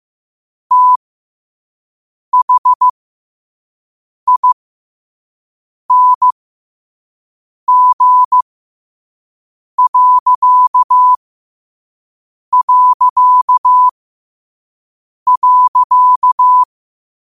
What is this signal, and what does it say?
Morse code, heard as a keyed tone, about 15 wpm, spaced as for 5 wpm, 1000 Hz. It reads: THING...
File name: qotdfiles / 5wpm